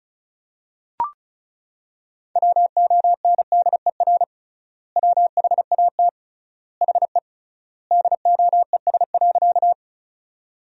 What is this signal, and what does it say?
Morse code, heard as a keyed tone, about 36 wpm, 700 Hz. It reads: WONDER WHAT HE DOES.
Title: 3x Word-Spacing - Sentences from Top 500 Words - 35wpm